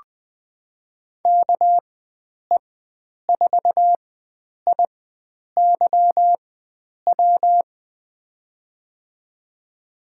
Morse code, heard as a keyed tone, about 20 wpm, spaced as for 9 wpm, 700 Hz. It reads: KE4IYW
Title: USA Call Signs - 6 Characters - 4x Character Spacing - 20wpm